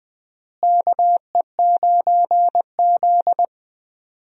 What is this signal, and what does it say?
Morse code, heard as a keyed tone, about 21 wpm, 700 Hz. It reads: KE9Z
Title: USA Call Signs - 4 Characters - Repeated 2x - 20wpm